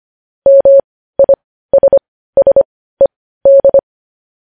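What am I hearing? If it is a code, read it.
Morse code, heard as a keyed tone, about 25 wpm, spaced as for 15 wpm, 550 Hz. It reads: MISSED
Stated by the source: Morse news headlines at 15 wpm.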